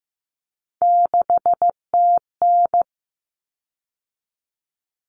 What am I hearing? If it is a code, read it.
Morse code, heard as a keyed tone, about 15 wpm, 700 Hz. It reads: 6TN
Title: Common Cut Numbers - 3-Digit 15wpm